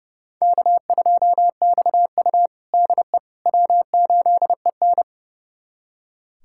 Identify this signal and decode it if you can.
Morse code, heard as a keyed tone, about 30 wpm, 700 Hz. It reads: K2XU DE W8EN
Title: USA Callsign DE Callsign - 4 Characters - 30wpm